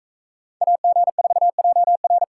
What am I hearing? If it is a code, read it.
Morse code, heard as a keyed tone, about 43 wpm, 700 Hz. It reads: AG4JR